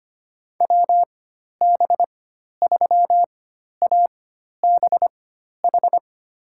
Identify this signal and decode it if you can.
Morse code, heard as a keyed tone, about 25 wpm, spaced as for 12 wpm, 700 Hz. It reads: WB3ABH
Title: USA Call Signs - 6 Characters - 4x Character Spacing - 25wpm